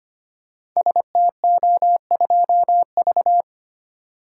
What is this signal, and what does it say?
Morse code, heard as a keyed tone, about 25 wpm, 700 Hz. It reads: STO2V
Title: Code Groups - 5 Letter:Numbers - 25wpm